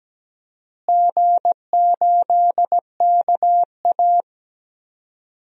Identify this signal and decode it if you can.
Morse code, heard as a keyed tone, about 17 wpm, 700 Hz. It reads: G8KA
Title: DXCC-100 Call Signs - 4 Characters - Repeated 3x - 17wpm